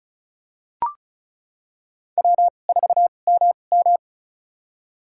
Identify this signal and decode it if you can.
Morse code, heard as a keyed tone, about 36 wpm, spaced as for 25 wpm, 700 Hz. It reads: W4MM
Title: USA Call Signs - 4 Characters - 2x Character Spacing - 35wpm